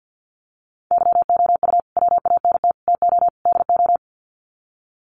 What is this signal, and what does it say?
Morse code, heard as a keyed tone, about 50 wpm, 700 Hz. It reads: YOU WANT TO DO
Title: Sets of 4 Words 50wpm